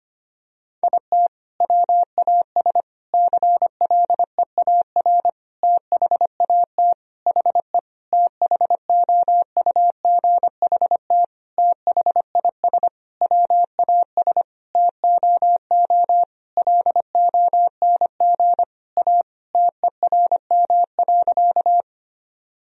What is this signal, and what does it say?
Morse code, heard as a keyed tone, about 25 wpm, 700 Hz. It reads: IT WAS CLEAR THAT HE THOUGHT THIS WAS TOO LONG A TERM.
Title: Sentences from Top 1000 Words - Review Part 4 - 25wpm